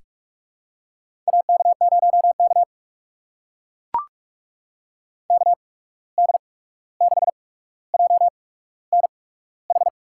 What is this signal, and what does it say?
Morse code, heard as a keyed tone, about 42 wpm, spaced as for 17 wpm, 700 Hz. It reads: AK0K KD6JNH
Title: USA Call Signs - 8x Character Spacing - 45wpm